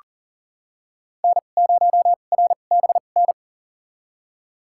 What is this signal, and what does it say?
Morse code, heard as a keyed tone, about 40 wpm, spaced as for 29 wpm, 700 Hz. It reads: N0RBN
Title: USA Call Signs - 5 Characters - 2x Character Spacing - 40wpm